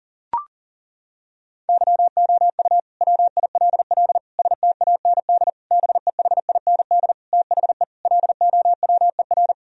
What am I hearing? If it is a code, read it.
Morse code, heard as a keyed tone, about 40 wpm, 700 Hz. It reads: YOU WILL STAND BEHIND THE LOWER
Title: Sentences from Top 1000 Words - Review Part 4 - 40wpm